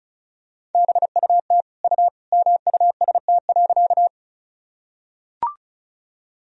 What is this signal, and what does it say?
Morse code, heard as a keyed tone, about 36 wpm, 700 Hz. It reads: BUT U MUST.
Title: Sentences from Top 200 Words and Abbreviations 35wpm